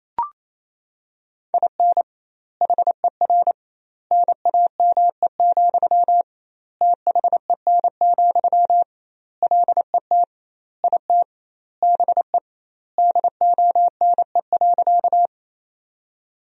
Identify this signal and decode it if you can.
Morse code, heard as a keyed tone, about 28 wpm, 700 Hz. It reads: IN HER NAME, THEN, LET IT BE DONE.
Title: 2x Word-Spacing - Sentences from Top 500 Words - 28wpm